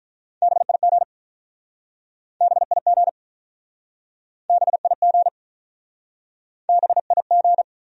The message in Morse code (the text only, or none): BIG BIG BIG BIG